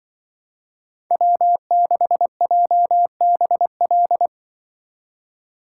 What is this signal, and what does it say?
Morse code, heard as a keyed tone, about 24 wpm, 700 Hz. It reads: W6JBL
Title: Speed Racing - USA 1x3 Call Signs 20wpm